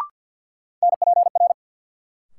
Morse code, heard as a keyed tone, about 52 wpm, 700 Hz. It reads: NPR
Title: Sets of 5 Words - New Word Drill - 50wpm